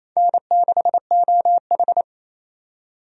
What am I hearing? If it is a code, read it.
Morse code, heard as a keyed tone, about 28 wpm, 700 Hz. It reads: N6OH
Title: USA Call Signs - 4 Characters - 28wpm